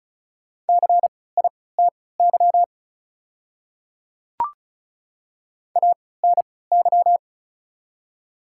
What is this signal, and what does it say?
Morse code, heard as a keyed tone, about 36 wpm, spaced as for 20 wpm, 700 Hz. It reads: CITY ANY
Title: 3x Character Spacing - Top 300 Words - 35wpm